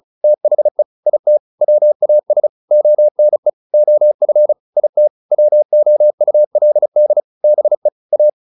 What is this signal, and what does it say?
Morse code, heard as a keyed tone, about 34 wpm, 600 Hz. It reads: THE IT WAS ONE OF IT WOULD BE A